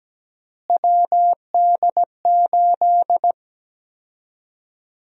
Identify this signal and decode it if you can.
Morse code, heard as a keyed tone, about 17 wpm, 700 Hz. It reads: WD8
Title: USA Call Sign Prefixes - Repeated 2x - 17wpm